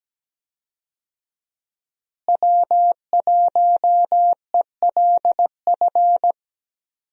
Morse code, heard as a keyed tone, about 17 wpm, 700 Hz. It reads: W1ELF
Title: USA Call Signs - 5 Characters - Repeated 2x - 17wpm